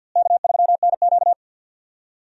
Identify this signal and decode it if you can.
Morse code, heard as a keyed tone, about 45 wpm, 700 Hz. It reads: K3NQ
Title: USA Call Signs - 4 Characters - Repeated 2x - 50wpm